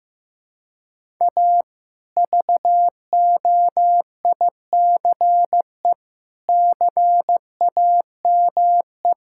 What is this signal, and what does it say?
Morse code, heard as a keyed tone, about 15 wpm, 700 Hz. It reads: A VOICE CAME
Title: Sentences from Top 1000 Words - Review Part 5 - 15wpm